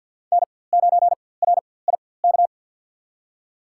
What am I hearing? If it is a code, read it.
Morse code, heard as a keyed tone, about 47 wpm, spaced as for 23 wpm, 700 Hz. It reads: N9RIK
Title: USA Call Signs - 5 Characters - 4x Character Spacing - 50wpm